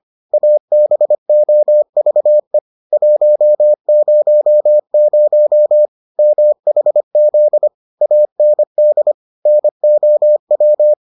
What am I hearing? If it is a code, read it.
Morse code, heard as a keyed tone, about 25 wpm, 600 Hz. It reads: ABOVE 100 MHZ AND NOW